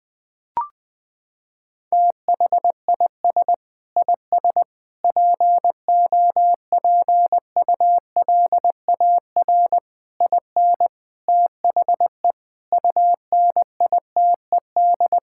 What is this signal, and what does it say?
Morse code, heard as a keyed tone, about 20 wpm, 700 Hz. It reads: THIS IS POPULAR IN THE UNITED
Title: Sentences with 5-Word Phrases - 20wpm